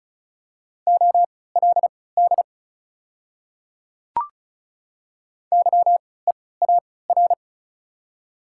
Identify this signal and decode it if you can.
Morse code, heard as a keyed tone, about 34 wpm, spaced as for 20 wpm, 700 Hz. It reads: OLD YEAR
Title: 3x Character Spacing - Top 200 Words - 35wpm